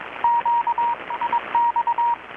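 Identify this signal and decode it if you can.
Morse code, heard as a keyed tone, about 22 wpm, 940 Hz, with radio noise. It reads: Q IX
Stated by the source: Morse from KPH's Night of Nights XX maritime radiotelegraphy event, received in Chiapas, Mexico about 3,800 km away on a Kenwood R-600 with a grounded 10 m long-wire antenna.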